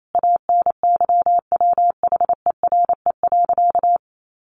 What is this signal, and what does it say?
Morse code, heard as a keyed tone, about 28 wpm, 700 Hz. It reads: ANYWHERE.
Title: Sentences from Top 1400 Words 28wpm